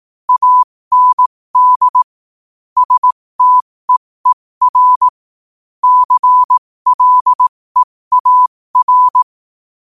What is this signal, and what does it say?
Morse code, heard as a keyed tone, about 18 wpm, spaced as for 15 wpm, 1000 Hz. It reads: AND STEER CLEAR
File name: 15wpm